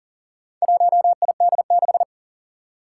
Morse code, heard as a keyed tone, about 37 wpm, 700 Hz. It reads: 1ID6
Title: Four Challenging Letter-Number Combinations 40wpm